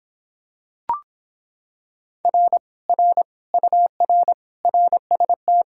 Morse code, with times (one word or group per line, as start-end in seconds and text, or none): 2.2-2.6 R
2.9-3.2 R
3.5-4.3 UR
4.6-5.6 RST